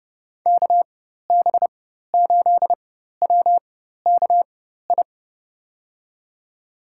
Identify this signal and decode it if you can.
Morse code, heard as a keyed tone, about 30 wpm, spaced as for 14 wpm, 700 Hz. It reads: KB8WKI
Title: USA Call Signs - 6 Characters - 4x Character Spacing - 30wpm